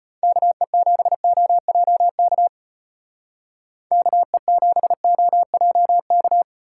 Morse code, heard as a keyed tone, about 31 wpm, 700 Hz. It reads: KE7OJK KE7OJK